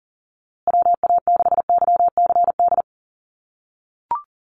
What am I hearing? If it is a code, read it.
Morse code, heard as a keyed tone, about 39 wpm, 700 Hz. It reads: WA6YCD
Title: USA Call Signs Repeated 2x 40wpm